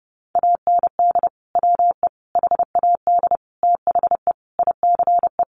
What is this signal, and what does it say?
Morse code, heard as a keyed tone, about 30 wpm, 700 Hz. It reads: AND WE HAD THE ICE
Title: Sentences from Top 1500 Words 30wpm